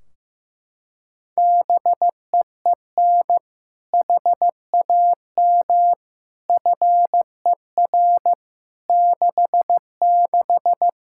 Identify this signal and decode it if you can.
Morse code, heard as a keyed tone, about 15 wpm, 700 Hz. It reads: BEEN HAM FER 66
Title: Practice QSO - Exchange 3 - More Info - 15wpm